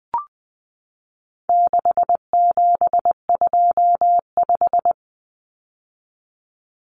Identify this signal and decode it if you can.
Morse code, heard as a keyed tone, about 20 wpm, 700 Hz. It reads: 6725
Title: Numbers - 4-Digit 20wpm